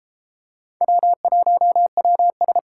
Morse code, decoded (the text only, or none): W1WS